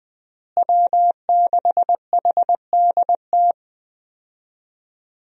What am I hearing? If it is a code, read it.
Morse code, heard as a keyed tone, about 20 wpm, 700 Hz. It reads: W6HDT
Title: USA Call Signs - 5 Characters - Repeated 3x - 20wpm